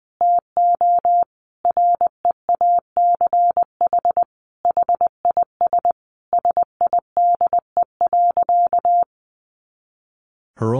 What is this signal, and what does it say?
Morse code, heard as a keyed tone, about 20 wpm, 700 Hz. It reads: TO REACH HIS SIDE.
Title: Sentences from Top 400 Words 20wpm